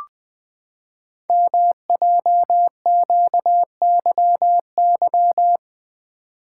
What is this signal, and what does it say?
Morse code, heard as a keyed tone, about 20 wpm, 700 Hz. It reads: MJQYY
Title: Code Groups - 5 Letters - 20wpm